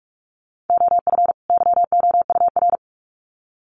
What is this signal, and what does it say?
Morse code, heard as a keyed tone, about 44 wpm, 700 Hz. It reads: OF YOUR
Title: Sets of 2 Words 45wpm